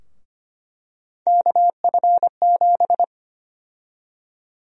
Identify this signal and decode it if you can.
Morse code, heard as a keyed tone, about 25 wpm, 700 Hz. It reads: KF7
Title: USA Call Sign Prefixes - Repeated 2x - 25wpm